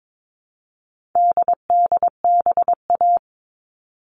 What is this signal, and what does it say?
Morse code, heard as a keyed tone, about 22 wpm, 700 Hz. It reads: DDBA
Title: All Cut Numbers - 4-Digit 22wpm